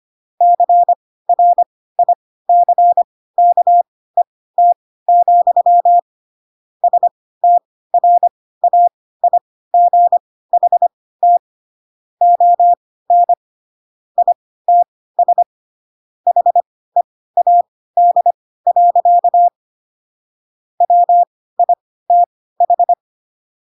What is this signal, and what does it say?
Morse code, heard as a keyed tone, about 25 wpm, spaced as for 16 wpm, 700 Hz. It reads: CRICKET, STRAIGHT ON ITS HEAD. WITH
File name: Pinocchio / Pinocchio ch4 10 25 ik1hge